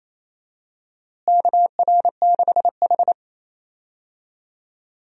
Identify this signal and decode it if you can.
Morse code, heard as a keyed tone, about 28 wpm, 700 Hz. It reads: KR6H
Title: USA Call Signs - 4 Characters - Repeated 3x - 28wpm